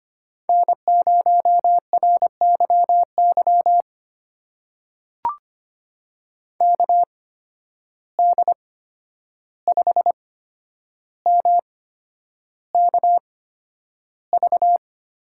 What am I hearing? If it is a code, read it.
Morse code, heard as a keyed tone, about 25 wpm, spaced as for 10 wpm, 700 Hz. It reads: N0RYY KD5MKV
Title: USA Call Signs - 8x Character Spacing - 25wpm